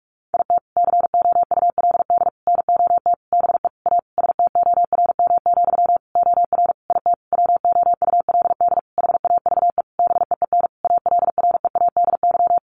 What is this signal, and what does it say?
Morse code, heard as a keyed tone, about 44 wpm, 700 Hz. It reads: IT COULD NOT BE A STORM, OR IT WOULD HAVE BEEN ALREADY
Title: Sentences from Top 1500 Words 45wpm